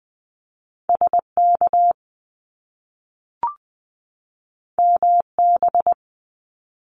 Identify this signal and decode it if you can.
Morse code, heard as a keyed tone, about 20 wpm, 700 Hz. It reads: SK MB
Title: Top Two Letter Combinations in Words 20wpm